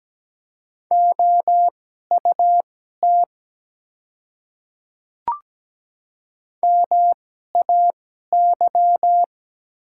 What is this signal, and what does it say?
Morse code, heard as a keyed tone, about 17 wpm, spaced as for 12 wpm, 700 Hz. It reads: OUT MAY
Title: 2x Character Spacing - Top 100 Words - 17wpm